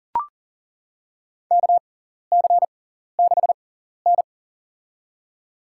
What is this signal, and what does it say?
Morse code, heard as a keyed tone, about 40 wpm, spaced as for 14 wpm, 700 Hz. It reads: KC6N
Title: USA Call Signs - 6x Character Spacing - 40wpm